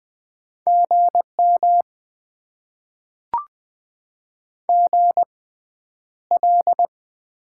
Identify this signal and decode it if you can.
Morse code, heard as a keyed tone, about 20 wpm, spaced as for 10 wpm, 700 Hz. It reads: GM GL
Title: Top 90 QSO Elements - 6x Character Spacing - 20wpm